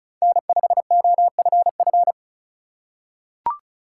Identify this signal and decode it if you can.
Morse code, heard as a keyed tone, about 36 wpm, 700 Hz. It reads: N5OFF
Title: USA Call Signs - 5 Characters - 2x Character Spacing - 35wpm